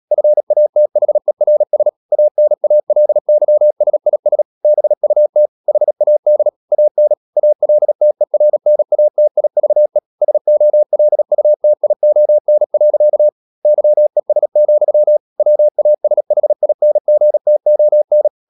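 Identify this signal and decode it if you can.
Morse code, heard as a keyed tone, about 37 wpm, 600 Hz. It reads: FATHERS ANALYSIS BUT HAD AN ALTERNATIVE SOLUTION. YES, WASHINGTON
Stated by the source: Morse code practice at 35–39 wpm sending English text of Ken Follett's novel Never, from CW med Gnister.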